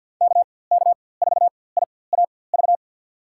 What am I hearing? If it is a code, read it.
Morse code, heard as a keyed tone, about 51 wpm, spaced as for 23 wpm, 700 Hz. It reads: KK4IAV
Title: USA Call Signs - 6 Characters - 4x Character Spacing - 50wpm